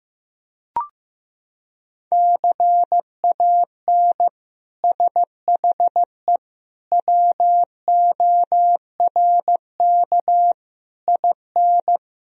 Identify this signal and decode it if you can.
Morse code, heard as a keyed tone, about 15 wpm, 700 Hz. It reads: CAN SHE WORK IN
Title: Sentences from Top 1000 Words - Review Part 4 - 15wpm